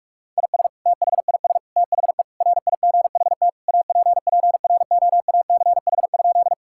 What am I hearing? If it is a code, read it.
Morse code, heard as a keyed tone, about 41 wpm, 700 Hz. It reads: IS THIS THE RIGHT APPROACH?
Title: Sentences from Top 1000 Words - Review Part 6 - 45wpm